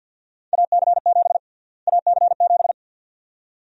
Morse code, heard as a keyed tone, about 48 wpm, 700 Hz. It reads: AC7 AC7
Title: USA Call Sign Prefixes - Repeated 2x - 50wpm